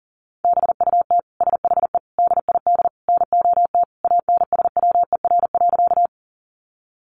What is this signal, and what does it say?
Morse code, heard as a keyed tone, about 40 wpm, 700 Hz. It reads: BUT SHE DID NOT ANSWER.
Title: Sentences from Top 600 Words 40wpm